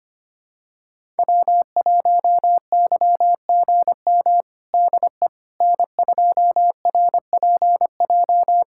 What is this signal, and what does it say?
Morse code, heard as a keyed tone, about 25 wpm, 700 Hz. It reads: W1YGM DE N2RPJ
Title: USA Callsign DE Callsign - 5 Characters - 25wpm